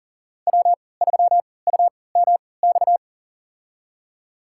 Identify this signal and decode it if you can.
Morse code, heard as a keyed tone, about 39 wpm, spaced as for 23 wpm, 700 Hz. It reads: W3UMX
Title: USA Call Signs - 5 Characters - 3x Character Spacing - 40wpm